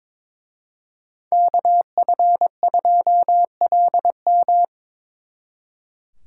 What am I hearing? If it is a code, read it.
Morse code, heard as a keyed tone, about 26 wpm, 700 Hz. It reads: KF2LM
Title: Speed Racing - USA 2x2 Call Signs 22wpm